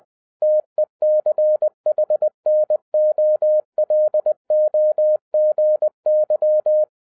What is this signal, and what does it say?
Morse code, heard as a keyed tone, about 20 wpm, 610 Hz. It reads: TECHNOLOGY